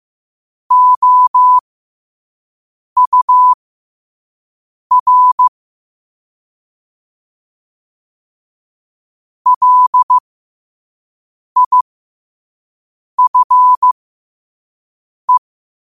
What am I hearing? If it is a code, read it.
Morse code, heard as a keyed tone, about 15 wpm, spaced as for 5 wpm, 1000 Hz. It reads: OUR LIFE
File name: qotdfiles / 5wpm